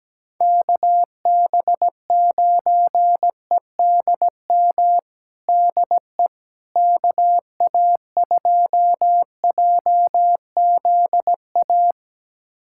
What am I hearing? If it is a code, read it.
Morse code, heard as a keyed tone, about 17 wpm, 700 Hz. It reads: KB9EDM DE KA2JZA
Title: USA Callsign DE Callsign 17wpm